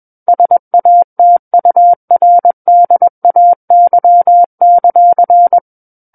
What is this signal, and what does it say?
Morse code, heard as a keyed tone, about 21 wpm, 700 Hz. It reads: SATURDAY;
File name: Book One / 12-Chapter 12 - 21 WPM